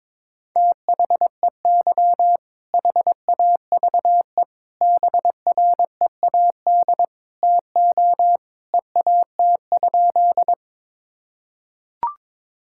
Morse code, heard as a keyed tone, about 22 wpm, 700 Hz. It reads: THEY HAVE BREAD TO EAT?
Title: Sentences from Top 1600 Words 22wpm